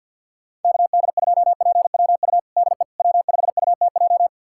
Morse code, heard as a keyed tone, about 51 wpm, 690 Hz. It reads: KD2PWU DE W5UTJ
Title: USA Callsign DE Callsign 50wpm